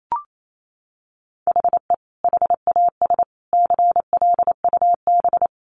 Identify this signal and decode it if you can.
Morse code, heard as a keyed tone, about 28 wpm, 700 Hz. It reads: HE HAS CLUB
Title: Sentences from Top 1200 Words 28wpm